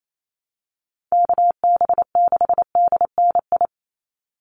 Morse code, heard as a keyed tone, about 28 wpm, 700 Hz. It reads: KB6DNI
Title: USA Call Signs 28wpm